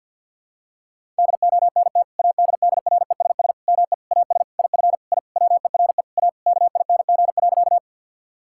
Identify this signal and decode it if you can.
Morse code, heard as a keyed tone, about 46 wpm, 700 Hz. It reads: DONT ADDRESS ME AS IF I WERE A KING.
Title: Sentences from Top 1000 Words - Review Part 6 - 50wpm